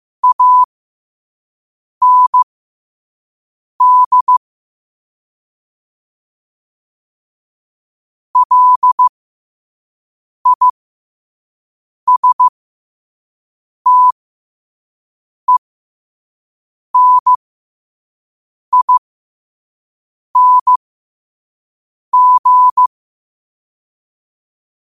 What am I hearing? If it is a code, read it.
Morse code, heard as a keyed tone, about 15 wpm, spaced as for 5 wpm, 1000 Hz. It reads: AND LISTENING
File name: qotdfiles / 5wpm